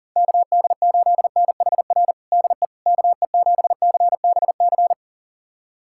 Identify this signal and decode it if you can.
Morse code, heard as a keyed tone, about 37 wpm, 700 Hz. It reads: KD8NHR DE KE7CBC
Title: USA Callsign DE Callsign 40wpm